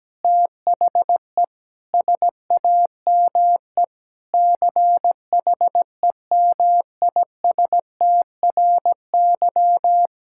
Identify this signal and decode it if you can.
Morse code, heard as a keyed tone, about 17 wpm, 700 Hz. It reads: THE SAME CHEMISTRY
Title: Sentences from Top 1000 Words - Review Part 6 - 17wpm